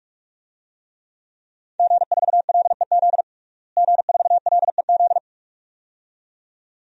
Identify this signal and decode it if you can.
Morse code, heard as a keyed tone, about 47 wpm, 700 Hz. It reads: G4LEZ G4LEZ
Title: DXCC-100 Call Signs - 5 Characters - Repeated 2x - 45wpm